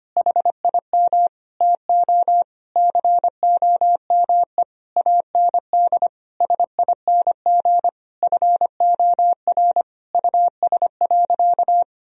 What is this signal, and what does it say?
Morse code, heard as a keyed tone, about 25 wpm, 700 Hz. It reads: HIM TO COME AND SING FOR US.
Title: Sentences from Top 1000 Words - Review Part 2 - 25wpm